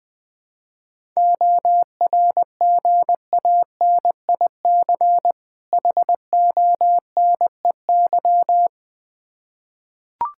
Morse code, heard as a keyed tone, about 20 wpm, 700 Hz. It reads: ORGANIC HONEY
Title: Difficult Words with Hints - 20wpm